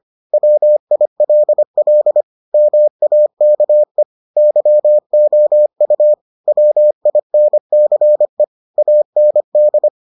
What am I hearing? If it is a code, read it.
Morse code, heard as a keyed tone, about 25 wpm, 600 Hz. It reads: WILL MAKE YOU WINCE AND